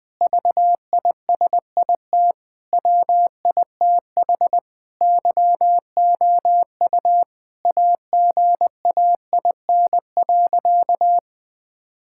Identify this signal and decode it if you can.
Morse code, heard as a keyed tone, about 20 wpm, 700 Hz. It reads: VISIT WITH YOU AGAIN.
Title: Sentences from Top 1000 Words - Review Part 6 - 20wpm